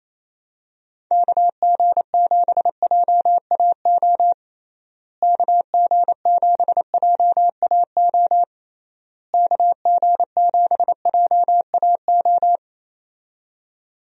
Morse code, heard as a keyed tone, about 27 wpm, 700 Hz. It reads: KG7JAO KG7JAO KG7JAO
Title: USA Call Signs - 6 Characters - Repeated 3x - 28wpm